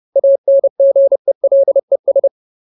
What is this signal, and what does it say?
Morse code, heard as a keyed tone, about 30 wpm, 550 Hz. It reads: ANGELES
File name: Ultimate Word List / 30WPM Ultimate